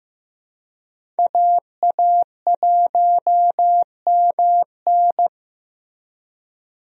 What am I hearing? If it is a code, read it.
Morse code, heard as a keyed tone, about 14 wpm, 700 Hz. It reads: AA1MN
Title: USA Call Signs - 5 Characters - Repeated 3x - 15wpm